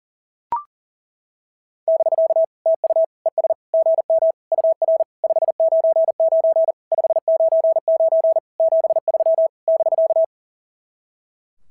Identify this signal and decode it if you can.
Morse code, heard as a keyed tone, about 40 wpm, 690 Hz. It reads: <BK> TU ES GM UR 599 599 73 <BK>
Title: SOTA QSO Practice 40wpm